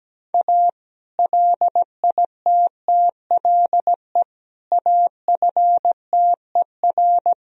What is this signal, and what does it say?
Morse code, heard as a keyed tone, about 17 wpm, 700 Hz. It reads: A LITTLE AFTER
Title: Sentences from Top 1000 Words - Review Part 2 - 17wpm